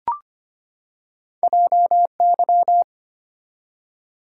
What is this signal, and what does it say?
Morse code, heard as a keyed tone, about 25 wpm, 700 Hz. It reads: JY
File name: Two Letter-Number Combinations - 25wpm